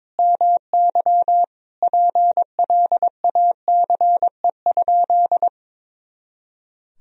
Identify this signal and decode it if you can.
Morse code, heard as a keyed tone, about 22 wpm, 700 Hz. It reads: MY PLACE?
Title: Sentences from Top 200 Words and Abbreviations 22wpm